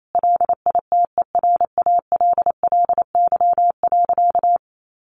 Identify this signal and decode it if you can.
Morse code, heard as a keyed tone, about 28 wpm, 700 Hz. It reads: LITERALLY.
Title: Sentences from Top 1400 Words 28wpm